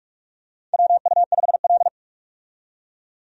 Morse code, heard as a keyed tone, about 42 wpm, 700 Hz. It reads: WU5L
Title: USA Call Signs - 4 Characters - 45wpm